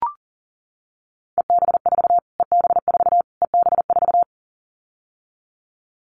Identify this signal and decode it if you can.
Morse code, heard as a keyed tone, about 40 wpm, 700 Hz. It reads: EB4 EB4 EB4
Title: Callsign Prefixes for Top 25 DXCC - Repeated 3x - 40wpm